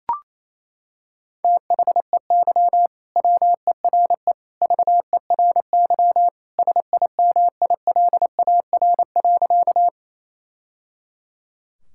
Morse code, heard as a keyed tone, about 28 wpm, 700 Hz. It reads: THEY WERE VERY SIMILAR.
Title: Sentences from Top 1000 Words - Review Part 5 - 28wpm